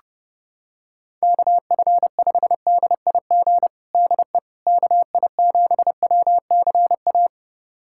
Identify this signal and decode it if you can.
Morse code, heard as a keyed tone, about 32 wpm, 700 Hz. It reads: KF5DIG DE KI7WCA
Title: USA Callsign DE Callsign 30wpm